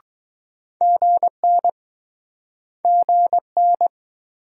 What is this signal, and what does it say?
Morse code, heard as a keyed tone, about 20 wpm, 700 Hz. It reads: GN GN